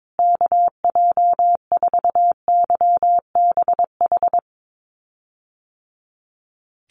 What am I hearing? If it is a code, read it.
Morse code, heard as a keyed tone, about 22 wpm, 700 Hz. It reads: KJ4YBH
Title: USA Call Signs Repeated 2x 22wpm